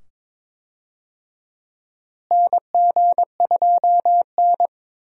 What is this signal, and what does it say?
Morse code, heard as a keyed tone, about 22 wpm, 700 Hz. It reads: NG2N
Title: USA Call Signs - 4 Characters - Repeated 2x - 22wpm